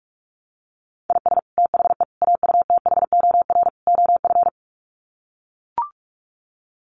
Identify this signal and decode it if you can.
Morse code, heard as a keyed tone, about 44 wpm, 700 Hz. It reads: IS THE AUTHOR OF
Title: Sets of 4 Words 45wpm